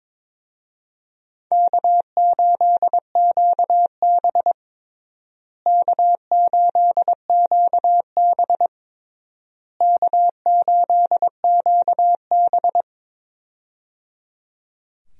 Morse code, heard as a keyed tone, about 22 wpm, 700 Hz. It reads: K8QB K8QB K8QB
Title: USA Call Signs - 4 Characters - Repeated 3x - 22wpm